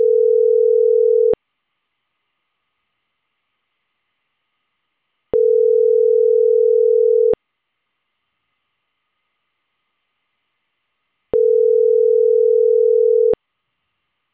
Ringback tone (North America), 440 + 480 Hz.